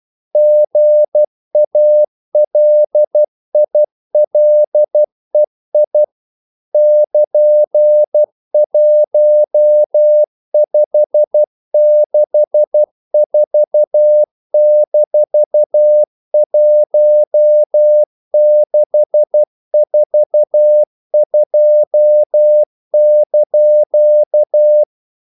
Morse code, heard as a keyed tone, about 12 wpm, 600 Hz. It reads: GALILEI <KN>1564-1642)
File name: CW / Practice / EnglishQuotations0112